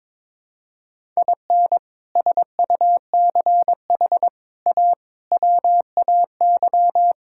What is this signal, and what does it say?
Morse code, heard as a keyed tone, about 22 wpm, 700 Hz. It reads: IN SUCH A WAY
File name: practice / 22wpm / Sets of 6 Words - 22wpm